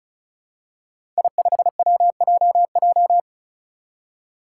Morse code, heard as a keyed tone, about 36 wpm, 700 Hz. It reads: I5WJJ
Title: Five Challenging Letter-Number Combinations 35wpm